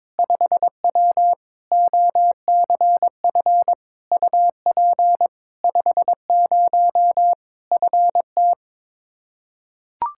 Morse code, heard as a keyed tone, about 22 wpm, 700 Hz. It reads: 5W OCF UP 50 FT
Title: Practice QSO - Exchange 2 - Info - 22wpm